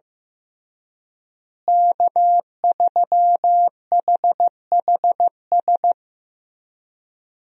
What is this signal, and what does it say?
Morse code, heard as a keyed tone, about 15 wpm, 700 Hz. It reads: K3HHS
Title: USA Call Signs - 5 Characters - Repeated 3x - 15wpm